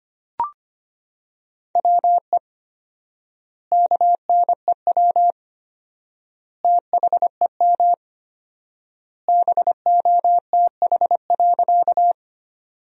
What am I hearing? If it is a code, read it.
Morse code, heard as a keyed tone, about 25 wpm, 700 Hz. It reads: WE KNEW THEM BOTH.
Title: 4x Word-Spacing - Sentences from Top 400 Words - 25wpm